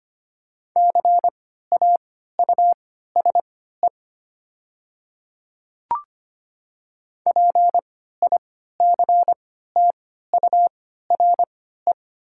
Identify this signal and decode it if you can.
Morse code, heard as a keyed tone, about 25 wpm, spaced as for 14 wpm, 700 Hz. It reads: CAUSE PICTURE
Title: 3x Character Spacing - Top 300 Words - 25wpm